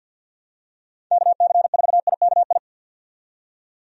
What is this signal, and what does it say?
Morse code, heard as a keyed tone, about 51 wpm, 700 Hz. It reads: KC4IKI
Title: USA Call Signs - 6 Characters - 50wpm